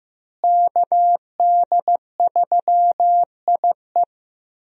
Morse code, heard as a keyed tone, about 15 wpm, 700 Hz. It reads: KD3IE
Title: USA Call Signs - 5 Characters - Repeated 3x - 15wpm